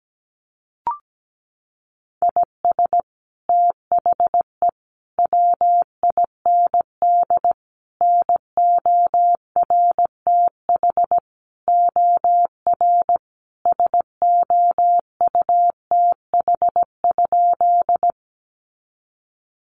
Morse code, heard as a keyed tone, about 17 wpm, 700 Hz. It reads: IS THE WIND NORTH OR SOUTH?
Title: Sentences from Top 400 Words 17wpm